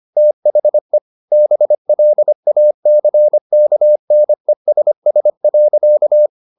Morse code, heard as a keyed tone, about 25 wpm, 600 Hz. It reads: THE BLACKNESS.
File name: the old man and the sea part 1 25wpm